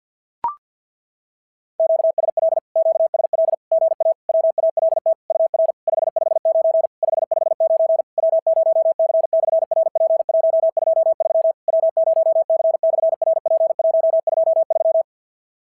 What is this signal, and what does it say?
Morse code, heard as a keyed tone, about 50 wpm, 650 Hz. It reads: QSL QSL GA WALT UR 559 559 W0C/RP123 W0C/RP123
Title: SOTA QSO Practice 50wpm